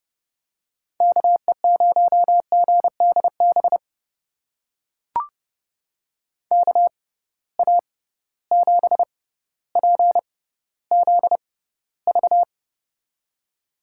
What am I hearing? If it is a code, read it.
Morse code, heard as a keyed tone, about 30 wpm, spaced as for 15 wpm, 700 Hz. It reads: KE0GDB KA7PZV
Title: USA Call Signs - 6 Characters - 6x Character Spacing - 30wpm